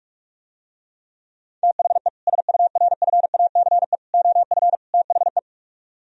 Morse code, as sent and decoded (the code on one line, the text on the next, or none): - .... / ... ..- .-. ..-. .- -.-. / --- ..-. / - .... .
THE SURFACE OF THE